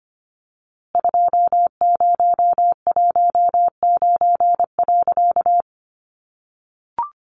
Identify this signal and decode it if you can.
Morse code, heard as a keyed tone, about 25 wpm, 700 Hz. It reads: 2019.
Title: A Sound Of Thunder 25wpm